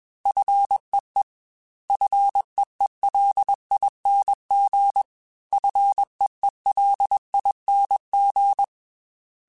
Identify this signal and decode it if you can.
Morse code, heard as a keyed tone, about 21 wpm, 800 Hz. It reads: FEE FEELING FEELING
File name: F and L / F and L mp3 / F and L Prefixes 3x2 Twice @20